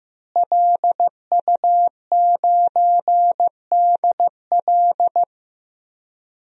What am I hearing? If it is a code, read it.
Morse code, heard as a keyed tone, about 15 wpm, 700 Hz. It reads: LU9DL
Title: DXCC-100 Call Signs - 5 Characters - Repeated 2x - 15wpm